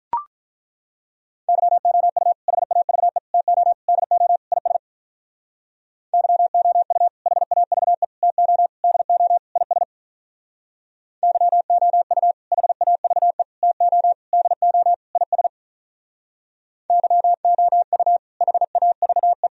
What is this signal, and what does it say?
Morse code, heard as a keyed tone, about 43 wpm, 700 Hz. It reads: YOU HAVE TO DO IS YOU HAVE TO DO IS YOU HAVE TO DO IS YOU HAVE
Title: Speed-Racing - Sets of 5 Words 35wpm